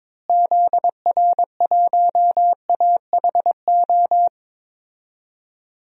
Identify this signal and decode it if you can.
Morse code, heard as a keyed tone, about 22 wpm, 700 Hz. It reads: ZR1AHO
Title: DXCC-100 Call Signs - 6 Characters - Repeated 2x - 22wpm